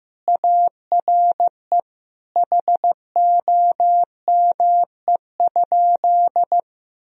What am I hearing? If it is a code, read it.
Morse code, heard as a keyed tone, about 15 wpm, 700 Hz. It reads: ARE HOME?
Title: Sentences from Top 1000 Words - Review Part 1 - 15wpm